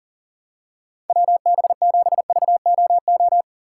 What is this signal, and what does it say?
Morse code, heard as a keyed tone, about 40 wpm, 700 Hz. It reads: WB7VOO
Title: USA Call Signs - 6 Characters - 2x Character Spacing - 40wpm